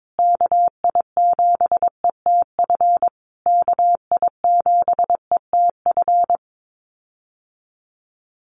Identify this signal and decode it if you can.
Morse code, heard as a keyed tone, about 22 wpm, 700 Hz. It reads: KI7ETF KI7ETF
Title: USA Call Signs Repeated 3x 22wpm